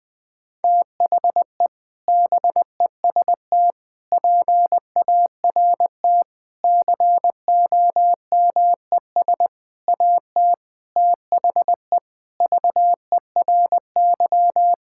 Morse code, heard as a keyed tone, about 20 wpm, 700 Hz. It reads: THE BEST PART COMES AT THE VERY
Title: Sentences with 6-Word Phrases - Spoken Beginning - 20wpm